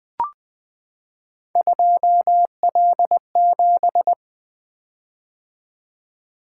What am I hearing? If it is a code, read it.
Morse code, heard as a keyed tone, about 20 wpm, 700 Hz. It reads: 2L7
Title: Three Letter-Number Combinations - 20wpm